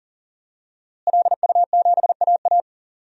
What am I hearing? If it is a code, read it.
Morse code, heard as a keyed tone, about 40 wpm, 700 Hz. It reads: LU7AA
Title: DXCC-100 Call Signs - 5 Characters - Repeated 3x - 40wpm